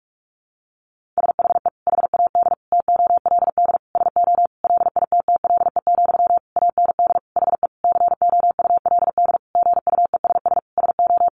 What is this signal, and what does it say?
Morse code, heard as a keyed tone, about 44 wpm, 700 Hz. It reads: SHE HAD TOLD SO LITTLE, AND HE COULD GUESS SO
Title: Sentences from Top 1500 Words 45wpm